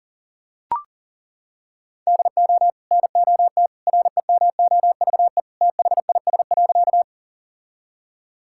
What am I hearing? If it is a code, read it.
Morse code, heard as a keyed tone, about 37 wpm, 700 Hz. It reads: DO NOT REMOVE THIS.
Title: Sentences from Top 1000 Words - Review Part 5 - 40wpm